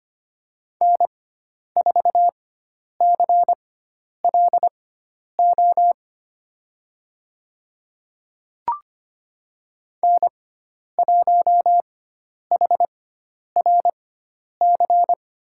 Morse code, heard as a keyed tone, about 25 wpm, spaced as for 10 wpm, 700 Hz. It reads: N4CLO N1HRC
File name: Lesson 16g - Letter c - Call Signs 25wpm